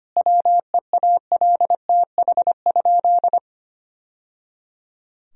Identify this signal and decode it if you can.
Morse code, heard as a keyed tone, about 25 wpm, 700 Hz. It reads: WEALTH?
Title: Sentences from Top 1000 Words - Review Part 4 - 25wpm